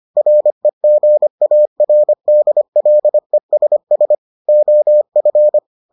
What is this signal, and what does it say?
Morse code, heard as a keyed tone, about 25 wpm, 600 Hz. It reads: REGARDLESS OF